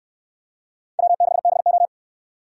Morse code, heard as a keyed tone, about 69 wpm, 700 Hz. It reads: W6BO